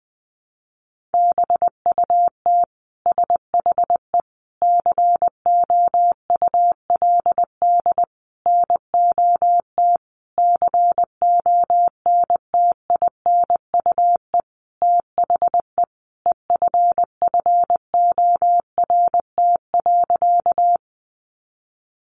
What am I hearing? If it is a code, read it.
Morse code, heard as a keyed tone, about 20 wpm, 700 Hz. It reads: BUT SHE COULD NOT CONTINUE THE EFFORT.
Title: Sentences from Top 1200 Words 20wpm